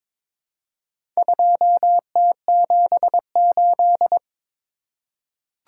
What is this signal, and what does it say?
Morse code, heard as a keyed tone, about 22 wpm, 700 Hz. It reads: 2T78
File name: practice / 22wpm / Four Letter-Number Combinations - 22wpm